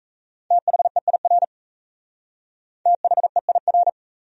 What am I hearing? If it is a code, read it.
Morse code, heard as a keyed tone, about 33 wpm, 700 Hz. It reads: THEIR THEIR
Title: Speed-Racing Top 100 Words 28wpm